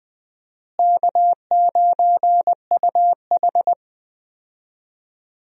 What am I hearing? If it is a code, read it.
Morse code, heard as a keyed tone, about 20 wpm, 700 Hz. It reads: K9UH
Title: USA Call Signs - 4 Characters - Repeated 2x - 20wpm